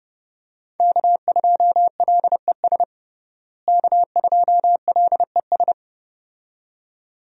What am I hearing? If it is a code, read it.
Morse code, heard as a keyed tone, about 32 wpm, 700 Hz. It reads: K2LES K2LES
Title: USA Call Signs - 5 Characters - Repeated 2x - 30wpm